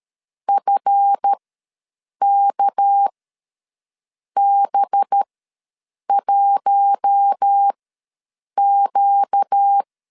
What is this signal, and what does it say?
Morse code, heard as a keyed tone, about 13 wpm, spaced as for 7 wpm, 800 Hz. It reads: FKB1Q
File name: LEZIONI / 7 GF9Z / 7 GF9Z ws 12wpm 3321